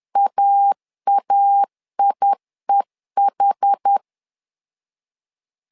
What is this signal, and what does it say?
Morse code, heard as a keyed tone, about 10 wpm, 800 Hz. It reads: AAIEH
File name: LEZIONI / 2 UHDN / 2 UH 10wpm 1714